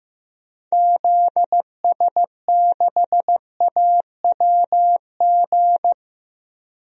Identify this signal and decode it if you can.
Morse code, heard as a keyed tone, about 15 wpm, 700 Hz. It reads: ZS6AWG
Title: DXCC-100 Call Signs - 6 Characters - 15wpm